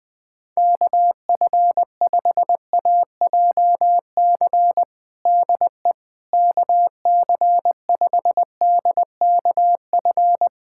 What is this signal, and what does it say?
Morse code, heard as a keyed tone, about 21 wpm, 700 Hz. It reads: KF5AJC DE KC5DKF
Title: USA Callsign DE Callsign - 6 Characters - 20wpm